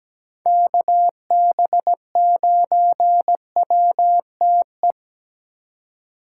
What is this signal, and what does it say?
Morse code, heard as a keyed tone, about 17 wpm, 700 Hz. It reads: KB9WTE
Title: USA Call Signs - 6 Characters - 3x Character Spacing - 17wpm